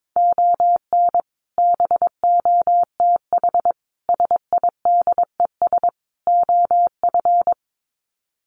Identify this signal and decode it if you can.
Morse code, heard as a keyed tone, about 22 wpm, 700 Hz. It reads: ON BOTH SIDES OF
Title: Sets of 4 Words 22wpm